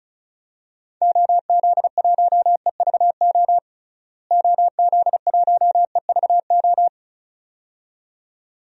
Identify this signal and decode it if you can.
Morse code, heard as a keyed tone, about 36 wpm, 700 Hz. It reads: OZ1EVO OZ1EVO
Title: DXCC-100 Call Signs - 6 Characters - Repeated 2x - 35wpm